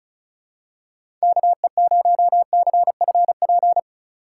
Morse code, heard as a keyed tone, about 36 wpm, 700 Hz. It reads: KE0CFP
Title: USA Call Signs - 6 Characters - 4x Character Spacing - 35wpm